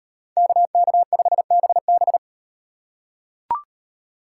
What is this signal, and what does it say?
Morse code, heard as a keyed tone, about 38 wpm, 700 Hz. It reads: KK5BB